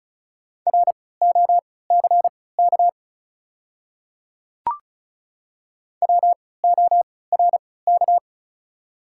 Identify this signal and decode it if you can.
Morse code, heard as a keyed tone, about 36 wpm, spaced as for 20 wpm, 700 Hz. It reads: ROCK WORK
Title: 3x Character Spacing - Top 500 Words - 35wpm